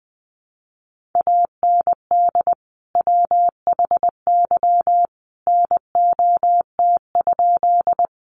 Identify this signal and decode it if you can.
Morse code, heard as a keyed tone, about 20 wpm, 700 Hz. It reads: AND WHY NOT?
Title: Sentences from Top 600 Words 20wpm